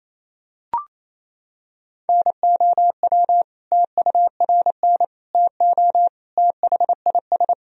Morse code, heard as a keyed tone, about 28 wpm, 700 Hz. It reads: NOW TURN TO THIS
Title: Sentences from Top 1000 Words - Review Part 1 - 28wpm